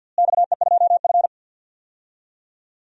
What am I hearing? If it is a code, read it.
Morse code, heard as a keyed tone, about 50 wpm, 700 Hz. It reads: XE2F
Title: DXCC-100 Call Signs - 4 Characters - Repeated 3x - 50wpm